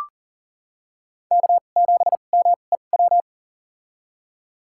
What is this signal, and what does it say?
Morse code, heard as a keyed tone, about 40 wpm, spaced as for 29 wpm, 700 Hz. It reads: K7MEW
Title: USA Call Signs - 5 Characters - 2x Character Spacing - 40wpm